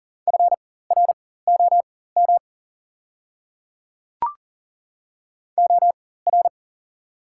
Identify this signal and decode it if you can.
Morse code, heard as a keyed tone, about 37 wpm, spaced as for 18 wpm, 700 Hz. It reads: FROM OR
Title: 4x Character Spacing - Top 200 Words - 40wpm